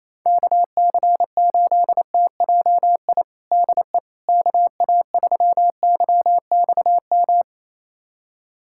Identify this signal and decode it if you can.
Morse code, heard as a keyed tone, about 28 wpm, 700 Hz. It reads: KC8TJI DE KA3YXM
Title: USA Callsign DE Callsign - 6 Characters - 28wpm